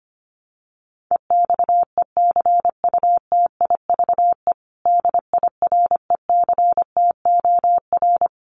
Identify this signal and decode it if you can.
Morse code, heard as a keyed tone, about 25 wpm, 700 Hz. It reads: EXECUTIVE DIRECTOR
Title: Sets of 5 Words Encore2 25wpm